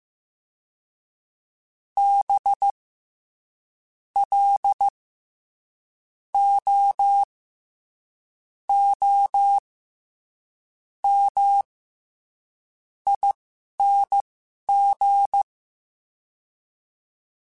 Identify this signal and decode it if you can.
Morse code, heard as a keyed tone, about 15 wpm, spaced as for 6 wpm, 800 Hz. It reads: BLOOMING
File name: Special INGs-mp3s / Special INGs-mp3s / SlowToFaster ing @10